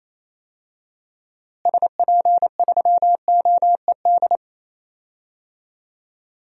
Morse code, heard as a keyed tone, about 28 wpm, 700 Hz. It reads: SP3OED
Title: DXCC-100 Call Signs - 6 Characters - Repeated 3x - 28wpm